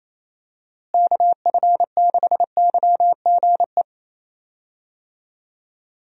Morse code, heard as a keyed tone, about 28 wpm, 700 Hz. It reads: KF6YGE
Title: USA Call Signs - 6 Characters - Repeated 2x - 28wpm